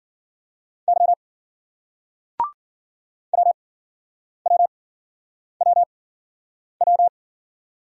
Morse code, heard as a keyed tone, about 53 wpm, spaced as for 9 wpm, 700 Hz. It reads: Y WWWW